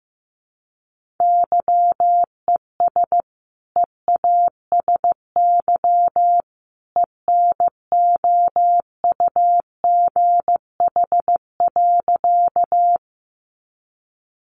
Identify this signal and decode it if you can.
Morse code, heard as a keyed tone, about 15 wpm, 700 Hz. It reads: YES EASY ENOUGH.
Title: Sentences from Top 700 Words 15wpm